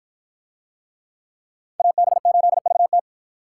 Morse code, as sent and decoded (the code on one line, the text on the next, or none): .- -... ---.. ...- -
AB8VT